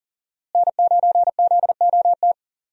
Morse code, heard as a keyed tone, about 40 wpm, 700 Hz. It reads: N9ZOT
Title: USA Call Signs - 5 Characters - Repeated 2x - 40wpm